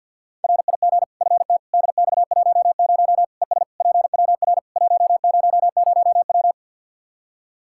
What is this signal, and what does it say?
Morse code, heard as a keyed tone, about 52 wpm, 700 Hz. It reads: RIG FT DX10 ES PWR 100W
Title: Practice QSO - Exchange 2 - Info - 50wpm